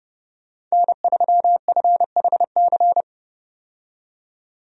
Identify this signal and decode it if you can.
Morse code, heard as a keyed tone, about 30 wpm, 700 Hz. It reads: N3FHC
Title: USA Call Signs - 5 Characters - 30wpm